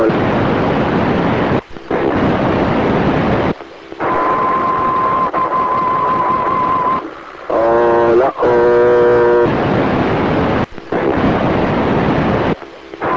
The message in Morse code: none